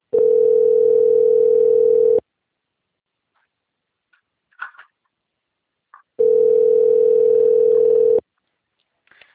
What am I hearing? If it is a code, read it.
Ringback tone (North America), 440 + 480 Hz.